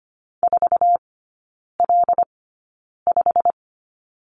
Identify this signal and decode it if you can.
Morse code, heard as a keyed tone, about 25 wpm, spaced as for 9 wpm, 700 Hz. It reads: 4L5
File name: Lesson 15e - Number 5 - Three Characters 25wpm